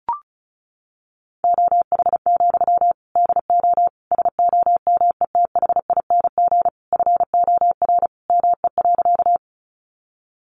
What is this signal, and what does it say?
Morse code, heard as a keyed tone, about 34 wpm, 700 Hz. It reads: OH, DO SOMETHING FOR ME.
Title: Sentences from Top 600 Words 35wpm